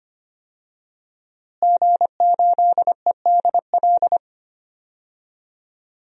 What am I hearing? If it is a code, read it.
Morse code, heard as a keyed tone, about 25 wpm, 700 Hz. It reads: G8EDL
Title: DXCC-100 Call Signs - 5 Characters - Repeated 2x - 25wpm